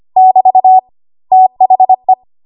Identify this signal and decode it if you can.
Morse code, heard as a keyed tone, about 25 wpm, 750 Hz. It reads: <BT> THE
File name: Archive / 25 WPM / 25WPM